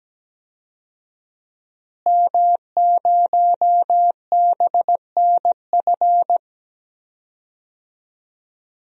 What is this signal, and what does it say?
Morse code, heard as a keyed tone, about 17 wpm, 700 Hz. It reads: M0BNF
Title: DXCC-100 Call Signs - 5 Characters - Repeated 2x - 17wpm